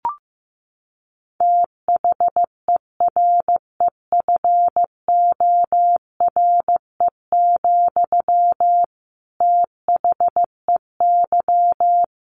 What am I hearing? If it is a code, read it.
Morse code, heard as a keyed tone, about 15 wpm, 700 Hz. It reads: THEREFORE, THEY
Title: Sentences from Top 1200 Words 15wpm